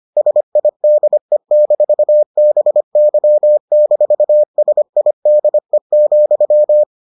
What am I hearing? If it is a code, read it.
Morse code, heard as a keyed tone, about 25 wpm, 600 Hz. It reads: SIDE-BY-SIDE,